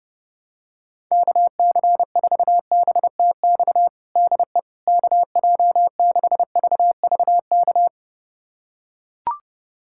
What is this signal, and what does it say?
Morse code, heard as a keyed tone, about 32 wpm, 700 Hz. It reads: KC4BTX DE KJ6VVK
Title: USA Callsign DE Callsign - 6 Characters - 30wpm